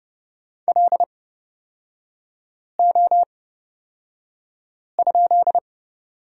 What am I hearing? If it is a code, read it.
Morse code, heard as a keyed tone, about 30 wpm, spaced as for 5 wpm, 700 Hz. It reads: LO?
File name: Lesson 36e - BT - Three Characters 30wpm